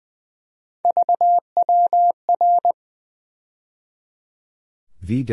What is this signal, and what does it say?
Morse code, heard as a keyed tone, about 20 wpm, 700 Hz. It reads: VWR
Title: USA Call Sign Postfixes - 3 Letters 20wpm